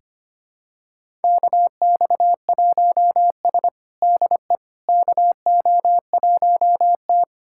Morse code, heard as a keyed tone, about 25 wpm, 700 Hz. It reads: KX1S DE KO1T
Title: USA Callsign DE Callsign - 4 Characters - 25wpm